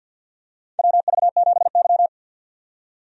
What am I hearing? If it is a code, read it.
Morse code, heard as a keyed tone, about 52 wpm, 700 Hz. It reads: WV7Y
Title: USA Call Signs - 4 Characters - 50wpm